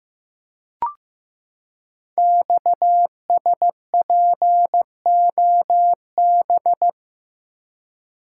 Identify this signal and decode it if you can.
Morse code, heard as a keyed tone, about 15 wpm, 700 Hz. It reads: XSPOB
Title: Code Groups - 5 Letters - 15wpm